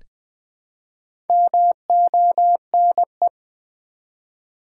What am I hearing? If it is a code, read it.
Morse code, heard as a keyed tone, about 20 wpm, 700 Hz. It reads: MONE
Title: Build A Word - 1 Letter At A Time - 5-Letter Words - 20wpm